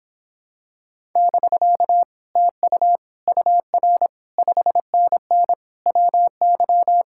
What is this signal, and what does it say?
Morse code, heard as a keyed tone, about 26 wpm, 700 Hz. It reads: <BK> TU UR 5NN WY